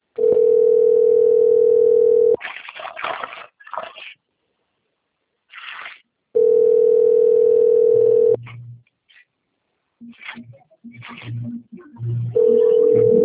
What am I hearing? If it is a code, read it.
Ringback tone (North America), 440 + 480 Hz.